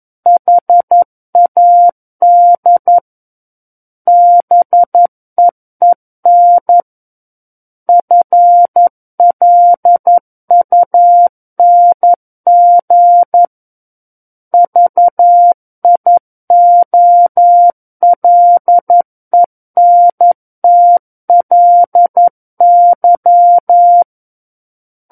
Morse code, heard as a keyed tone, about 11 wpm, 700 Hz. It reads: HAD BEEN FLUNG VIOLENTLY
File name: Book One / 02-Chapter 2 - 11 WPM